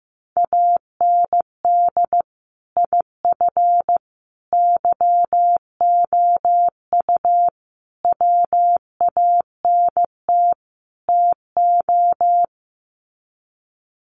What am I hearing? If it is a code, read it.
Morse code, heard as a keyed tone, about 15 wpm, 700 Hz. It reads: AND IF YOU WANT TO
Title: Sets of 5 Words Encore 15wpm